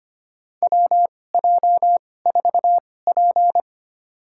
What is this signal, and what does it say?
Morse code, heard as a keyed tone, about 25 wpm, spaced as for 18 wpm, 700 Hz. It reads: WJ4P
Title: USA Call Signs - 4 Characters - 2x Character Spacing - 25wpm